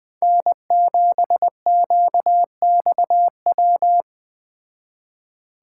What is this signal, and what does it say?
Morse code, heard as a keyed tone, about 20 wpm, 700 Hz. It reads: N7QXW